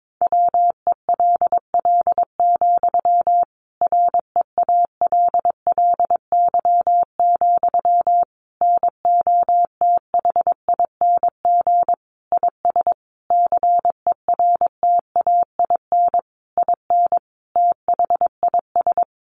Morse code, heard as a keyed tone, about 22 wpm, 700 Hz. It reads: WELL, REALLY, NOTHING IS CERTAIN IN THIS
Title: Sentences from Top 600 Words 22wpm